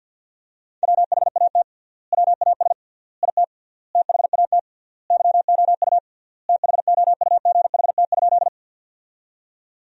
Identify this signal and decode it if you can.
Morse code, heard as a keyed tone, about 50 wpm, 700 Hz. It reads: WHAT WAS IT THAT YOU THOUGHT?
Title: 3x Word-Spacing - Sentences from Top 300 Words - 50wpm